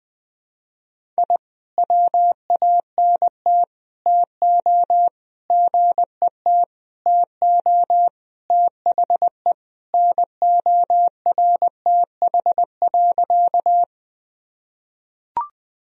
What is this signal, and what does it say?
Morse code, heard as a keyed tone, about 20 wpm, 700 Hz. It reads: I WANT TO GET TO THE NORTH.
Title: Sentences from Top 1000 Words - Review Part 1 - 20wpm